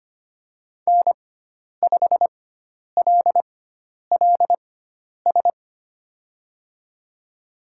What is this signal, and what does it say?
Morse code, heard as a keyed tone, about 25 wpm, spaced as for 10 wpm, 700 Hz. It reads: N5LLS
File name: Lesson 16g - Letter c - Call Signs 25wpm